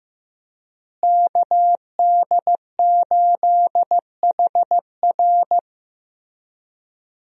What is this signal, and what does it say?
Morse code, heard as a keyed tone, about 15 wpm, 700 Hz. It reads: KD8HR
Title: USA Call Signs - 5 Characters - Repeated 3x - 15wpm